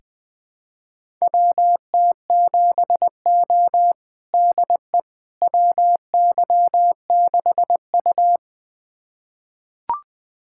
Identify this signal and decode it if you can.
Morse code, heard as a keyed tone, about 20 wpm, 700 Hz. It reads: WT7O DE WY6U
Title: USA Callsign DE Callsign - 4 Characters - 20wpm